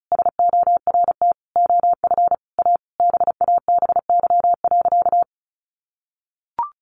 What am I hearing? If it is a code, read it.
Morse code, heard as a keyed tone, about 34 wpm, 700 Hz. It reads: SORT OF A BABY.
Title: Sentences from Top 1400 Words 35wpm